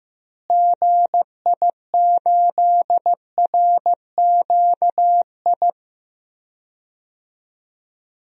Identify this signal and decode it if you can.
Morse code, heard as a keyed tone, about 15 wpm, 700 Hz. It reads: GI8RQI
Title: DXCC-100 Call Signs - 6 Characters - Repeated 2x - 15wpm